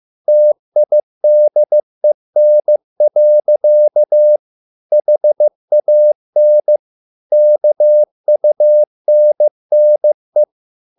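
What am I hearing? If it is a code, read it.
Morse code, heard as a keyed tone, about 15 wpm, 600 Hz. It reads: TIDEN. HAN KUNNE
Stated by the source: Morse code at a slow 15 wpm from CW med Gnister.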